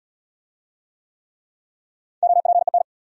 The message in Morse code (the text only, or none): OPEN